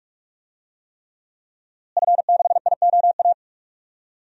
Morse code, heard as a keyed tone, about 42 wpm, 700 Hz. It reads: F6IOA
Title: DXCC-100 Call Signs - 5 Characters - Repeated 3x - 45wpm